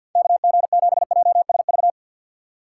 Morse code, heard as a keyed tone, about 49 wpm, 700 Hz. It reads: KG7JSV
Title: Speed Racing - USA 2x3 Call Signs 50wpm